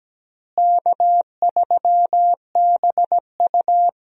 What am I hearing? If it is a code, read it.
Morse code, heard as a keyed tone, about 17 wpm, 700 Hz. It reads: K3BU
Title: USA Call Signs - 4 Characters - Repeated 3x - 17wpm